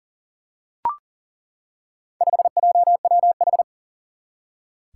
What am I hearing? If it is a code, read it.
Morse code, heard as a keyed tone, about 38 wpm, 700 Hz. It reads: 5JWH
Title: Four Challenging Letter-Number Combinations 40wpm